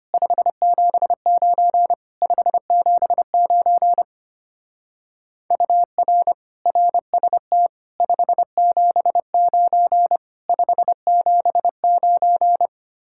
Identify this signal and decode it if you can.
Morse code, heard as a keyed tone, about 30 wpm, 700 Hz. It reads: 579 579 UR RST 579 579
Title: Speed-Racing Common QSO Phrases - 25wpm